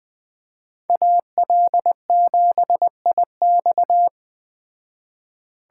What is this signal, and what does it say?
Morse code, heard as a keyed tone, about 20 wpm, 700 Hz. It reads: AL7IX
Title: Speed Racing - USA Call Signs 17wpm